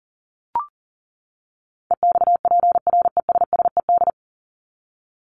Morse code, heard as a keyed tone, about 40 wpm, 700 Hz. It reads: EXPRESSED
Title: ED Suffix - Top 100 Words with Base Words 40wpm